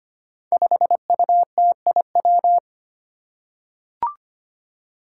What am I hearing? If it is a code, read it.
Morse code, heard as a keyed tone, about 25 wpm, 700 Hz. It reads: 5UTIW
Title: Five Challenging Letter-Number Combinations 25wpm